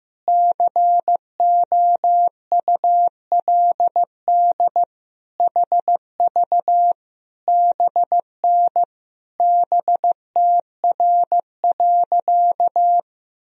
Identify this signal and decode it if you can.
Morse code, heard as a keyed tone, about 15 wpm, 700 Hz. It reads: COULD HV BN BTR.
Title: Sentences from Top 500 Words and Abbreviations 15wpm